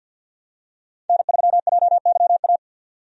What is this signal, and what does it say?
Morse code, heard as a keyed tone, about 51 wpm, 700 Hz. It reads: N3JYA
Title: USA Call Signs - 5 Characters - 50wpm